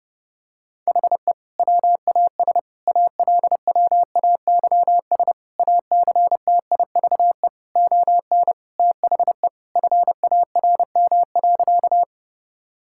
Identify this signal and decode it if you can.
Morse code, heard as a keyed tone, about 30 wpm, 700 Hz. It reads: HE WAS ALWAYS ACTIVE ON THE FARM.
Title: Sentences from Top 1000 Words - Review Part 6 - 30wpm